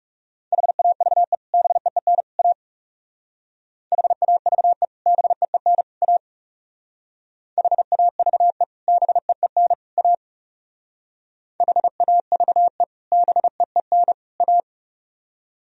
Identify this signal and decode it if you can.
Morse code, heard as a keyed tone, about 37 wpm, 700 Hz. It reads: HAVE BEEN A HAVE BEEN A HAVE BEEN A HAVE BEEN A
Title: Speed-Racing - Sets of 3 Words 30wpm